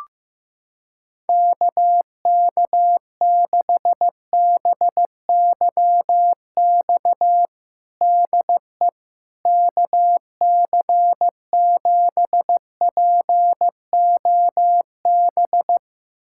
Morse code, heard as a keyed tone, about 15 wpm, 700 Hz. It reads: KK6BYX DE KC7POB
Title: USA Callsign DE Callsign - 6 Characters - 15wpm